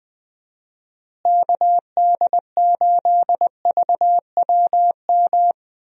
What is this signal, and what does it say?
Morse code, heard as a keyed tone, about 20 wpm, 700 Hz. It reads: KD8VWM
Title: USA Call Signs - 6 Characters - Repeated 2x - 20wpm